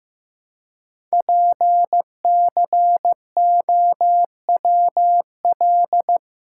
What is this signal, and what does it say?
Morse code, heard as a keyed tone, about 15 wpm, 700 Hz. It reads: PCOWL
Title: Code Groups - 5 Letters - 15wpm